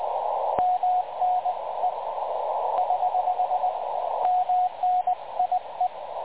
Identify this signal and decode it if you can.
Morse code, heard as a keyed tone, about 20 wpm, 730 Hz, with radio noise. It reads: MNE <HH> MNIE
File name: dlazaawansowanych / nagraniazpasma / websdr recording 3565.0kHz